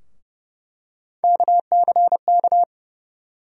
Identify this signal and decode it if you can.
Morse code, heard as a keyed tone, about 31 wpm, 700 Hz. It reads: KCK
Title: USA Call Sign Postfixes - 3 Letters 30wpm